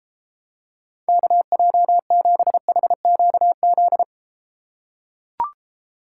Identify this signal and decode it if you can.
Morse code, heard as a keyed tone, about 31 wpm, 700 Hz. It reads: KJ7HQZ